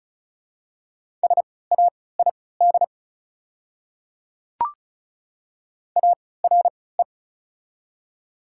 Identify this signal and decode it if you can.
Morse code, heard as a keyed tone, about 36 wpm, spaced as for 20 wpm, 700 Hz. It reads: SAID ARE
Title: 3x Character Spacing - Top 300 Words - 35wpm